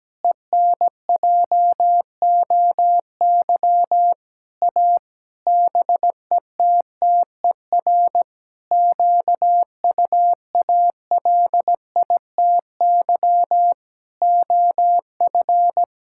Morse code, heard as a keyed tone, about 17 wpm, 700 Hz. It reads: ENJOY A BETTER QUALITY OF
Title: Sentences from Top 1000 Words - Review Part 5 - 17wpm